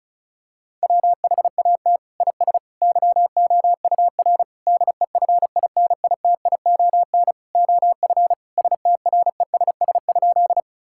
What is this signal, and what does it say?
Morse code, heard as a keyed tone, about 36 wpm, 700 Hz. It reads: WHAT IS YOUR DEFINITION OF STRESS?
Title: Sentences from Top 1000 Words - Review Part 6 - 35wpm